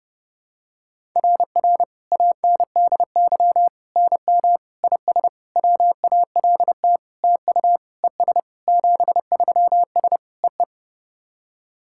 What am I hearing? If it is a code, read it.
Morse code, heard as a keyed tone, about 30 wpm, 700 Hz. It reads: RR ANDY NM IS WALT TU ES 73S EE
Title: SOTA QSO Practice 30wpm